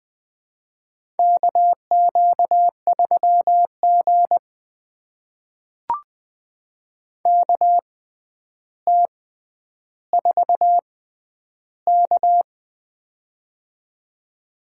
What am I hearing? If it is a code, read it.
Morse code, heard as a keyed tone, about 20 wpm, spaced as for 10 wpm, 700 Hz. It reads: KQ3G KT4K
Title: USA Call Signs - 4 Characters - 6x Character Spacing - 20wpm